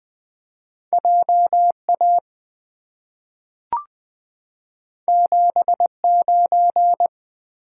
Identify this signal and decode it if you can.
Morse code, heard as a keyed tone, about 20 wpm, 700 Hz. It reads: J A 7 9
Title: Two Letter-Number Combinations - 20wpm